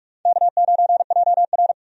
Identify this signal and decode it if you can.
Morse code, heard as a keyed tone, about 43 wpm, 700 Hz. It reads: K9JR
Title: USA Call Signs - 4 Characters - 45wpm